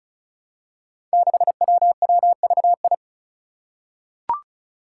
Morse code, heard as a keyed tone, about 36 wpm, 700 Hz. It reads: 6WWVI
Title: Five Challenging Letter-Number Combinations 35wpm